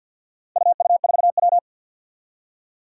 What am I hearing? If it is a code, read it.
Morse code, heard as a keyed tone, about 47 wpm, 700 Hz. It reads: UU4W
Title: Four Challenging Letter-Number Combinations 50wpm